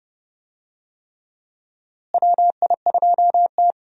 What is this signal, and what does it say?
Morse code, heard as a keyed tone, about 31 wpm, 700 Hz. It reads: WI2T
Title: USA Call Signs - 4 Characters - Repeated 3x - 30wpm